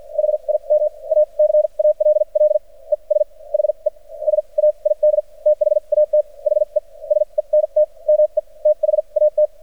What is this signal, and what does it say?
Morse code, heard as a keyed tone, about 53 wpm, spaced as for 39 wpm, 600 Hz. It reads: HIM A CALL ES HE SAID THAT HE SENT ME THAT
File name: archived / CRSnet